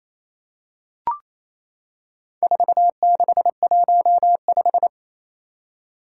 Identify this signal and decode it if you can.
Morse code, heard as a keyed tone, about 27 wpm, 700 Hz. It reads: 4615
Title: Four Challenging Letter-Number Combinations 28wpm